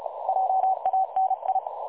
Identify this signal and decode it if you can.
Morse code, heard as a keyed tone, about 33 wpm, 760 Hz. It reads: JANI
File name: dlazaawansowanych / nagraniazpasma / websdr recording 3565.0kHz